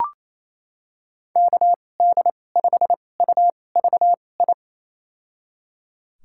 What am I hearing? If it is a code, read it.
Morse code, heard as a keyed tone, about 28 wpm, spaced as for 20 wpm, 700 Hz. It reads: KD5UVI
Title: USA Call Signs - 6 Characters - 2x Character Spacing - 28wpm